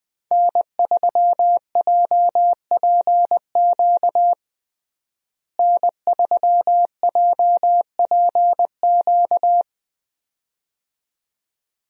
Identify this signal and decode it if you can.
Morse code, heard as a keyed tone, about 21 wpm, 700 Hz. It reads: N3JPQ N3JPQ
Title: USA Call Signs - 5 Characters - Repeated 2x - 20wpm